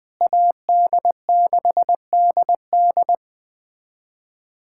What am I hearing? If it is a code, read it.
Morse code, heard as a keyed tone, about 21 wpm, 700 Hz. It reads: AD6DD
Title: USA Callsign DE Callsign - 5 Characters - 20wpm